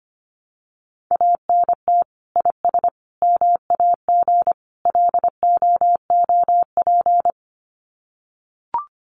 Morse code, heard as a keyed tone, about 25 wpm, 700 Hz. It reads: ANT IS MAG LOOP
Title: Common QSO Phrases 25wpm